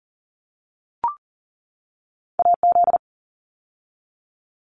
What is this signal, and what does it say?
Morse code, heard as a keyed tone, about 40 wpm, 700 Hz. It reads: AZ
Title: ARRL Sections 40wpm